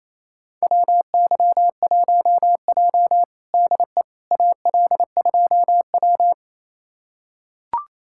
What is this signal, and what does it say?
Morse code, heard as a keyed tone, about 28 wpm, 700 Hz. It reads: WY1J DE AL2W
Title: USA Callsign DE Callsign - 4 Characters - 28wpm